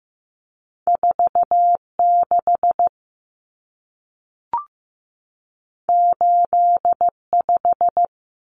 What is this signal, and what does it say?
Morse code, heard as a keyed tone, about 15 wpm, 700 Hz. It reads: 46 85
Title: Common Cut Numbers - 2-Digit 15wpm